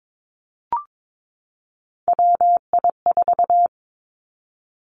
Morse code, heard as a keyed tone, about 22 wpm, 700 Hz. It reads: WI4
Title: Three Challenging Letter-Number Combinations 22wpm